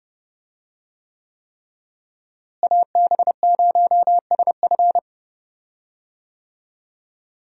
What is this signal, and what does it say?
Morse code, heard as a keyed tone, about 31 wpm, 700 Hz. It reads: AB0SF
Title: USA Call Signs - 5 Characters - Repeated 3x - 30wpm